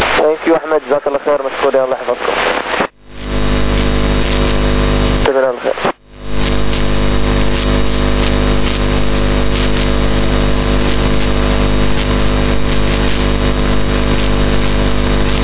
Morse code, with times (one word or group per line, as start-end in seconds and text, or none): none